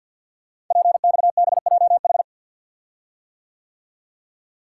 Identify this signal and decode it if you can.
Morse code, heard as a keyed tone, about 50 wpm, 700 Hz. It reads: PXBJH
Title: Code Groups - 5 Letters - 50wpm